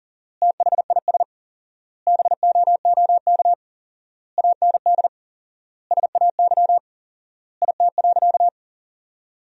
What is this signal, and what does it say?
Morse code, heard as a keyed tone, about 40 wpm, 700 Hz. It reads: THIS BOOK AND SAY IT.
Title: 4x Word-Spacing - Sentences from Top 400 Words - 40wpm